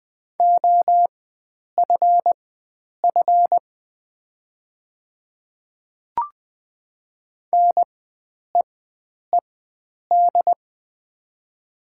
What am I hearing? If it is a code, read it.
Morse code, heard as a keyed tone, about 20 wpm, spaced as for 9 wpm, 700 Hz. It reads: OFF NEED
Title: 4x Character Spacing - Top 500 Words - 20wpm